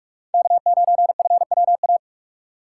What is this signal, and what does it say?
Morse code, heard as a keyed tone, about 44 wpm, 700 Hz. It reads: K9FWA